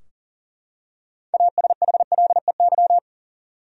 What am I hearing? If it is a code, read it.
Morse code, heard as a keyed tone, about 39 wpm, 700 Hz. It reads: ASHLEY
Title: USA Common Names - 40wpm